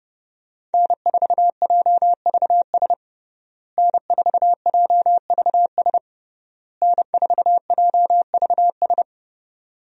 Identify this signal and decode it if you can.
Morse code, heard as a keyed tone, about 32 wpm, 700 Hz. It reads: N4JVS N4JVS N4JVS
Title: USA Call Signs - 5 Characters - Repeated 3x - 30wpm